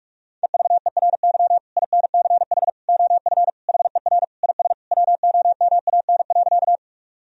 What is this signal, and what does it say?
Morse code, heard as a keyed tone, about 41 wpm, 700 Hz. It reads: EVERY INCH OF HER IS WOMAN.
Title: Sentences from Top 1000 Words - Review Part 3 - 45wpm